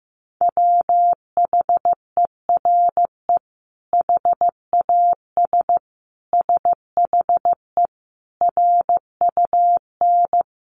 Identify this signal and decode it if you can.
Morse code, heard as a keyed tone, about 15 wpm, 700 Hz. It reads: WHERE HAS SHE RUN
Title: Sentences from Top 300 Words 15wpm